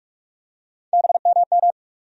Morse code, heard as a keyed tone, about 42 wpm, 700 Hz. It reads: BMM
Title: USA Call Sign Postfixes - 3 Letters 45wpm